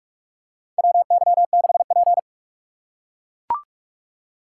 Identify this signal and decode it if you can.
Morse code, heard as a keyed tone, about 44 wpm, 700 Hz. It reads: WY6P
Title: USA Call Signs - 4 Characters - 6x Character Spacing - 45wpm